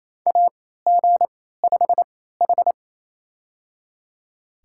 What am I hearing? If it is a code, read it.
Morse code, heard as a keyed tone, about 28 wpm, spaced as for 16 wpm, 700 Hz. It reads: AG5H
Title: USA Call Signs - 4 Characters - 3x Character Spacing - 28wpm